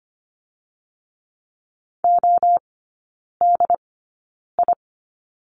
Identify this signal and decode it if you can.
Morse code, heard as a keyed tone, about 25 wpm, spaced as for 9 wpm, 690 Hz. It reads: ODI
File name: Lesson 15e - Number 5 - Three Characters 25wpm